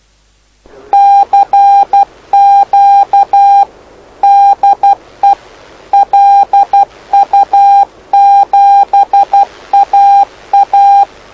Morse code, heard as a keyed tone, about 12 wpm, 800 Hz. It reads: CQ DE LU7AA
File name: lu4aao / lu7aa 09dic2011 exp planeador vuelo1